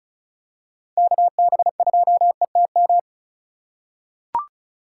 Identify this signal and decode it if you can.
Morse code, heard as a keyed tone, about 34 wpm, 700 Hz. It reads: KB2ETM
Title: USA Call Signs - 6 Characters - 2x Character Spacing - 35wpm